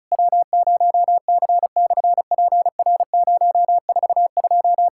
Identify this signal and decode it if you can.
Morse code, heard as a keyed tone, about 34 wpm, 700 Hz. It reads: W0C/PR042
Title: SOTA QSO Practice 35wpm